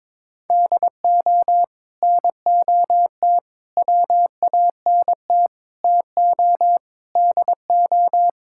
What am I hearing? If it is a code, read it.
Morse code, heard as a keyed tone, about 22 wpm, 700 Hz. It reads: DO NOT WANT TO DO
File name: practice / 22wpm / Build A Phrase - 5-Word Phrases - 22wpm